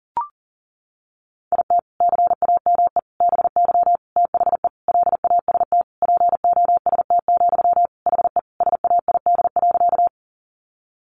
Morse code, heard as a keyed tone, about 40 wpm, 700 Hz. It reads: IT CAME BY THE LAST POST, HE SAID.
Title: Sentences from Top 1000 Words 40wpm